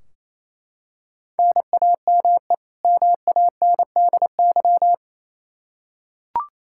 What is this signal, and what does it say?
Morse code, heard as a keyed tone, about 28 wpm, 700 Hz. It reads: NAME MANDY
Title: Practice QSO - Exchange 1 - Beginning - 28wpm